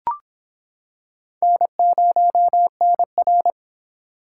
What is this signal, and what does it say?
Morse code, heard as a keyed tone, about 26 wpm, 700 Hz. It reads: N0NR